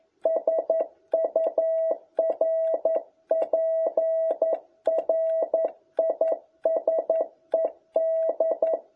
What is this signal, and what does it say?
Morse code, heard as a keyed tone, about 11 wpm, 650 Hz. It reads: SURPRISED
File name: jh8cbh / 2021cw